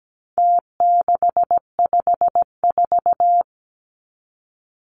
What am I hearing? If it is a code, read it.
Morse code, heard as a keyed tone, about 17 wpm, 700 Hz. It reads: T654
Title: Numbers - Contest Style - 4-Digit 17wpm